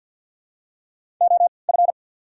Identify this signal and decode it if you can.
Morse code, heard as a keyed tone, about 53 wpm, spaced as for 29 wpm, 690 Hz. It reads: OF
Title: 3x Character Spacing - Top 300 Words - 50wpm